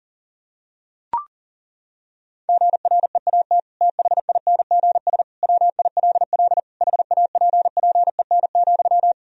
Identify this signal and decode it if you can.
Morse code, heard as a keyed tone, about 40 wpm, 700 Hz. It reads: GREAT THINGS WILL HAPPEN,
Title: Sentences from Top 1000 Words - Review Part 4 - 40wpm